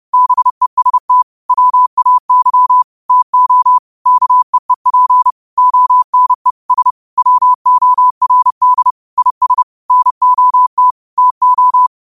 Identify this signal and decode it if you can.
Morse code, heard as a keyed tone, about 30 wpm, 1000 Hz. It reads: BEST WAY TO KEEP ONE S WORD IS NOT TO